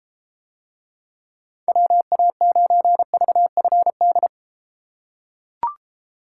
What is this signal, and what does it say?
Morse code, heard as a keyed tone, about 32 wpm, 700 Hz. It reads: WA9VFD